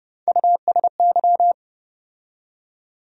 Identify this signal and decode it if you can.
Morse code, heard as a keyed tone, about 30 wpm, 700 Hz. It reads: USY
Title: USA Call Sign Postfixes - 3 Letters 30wpm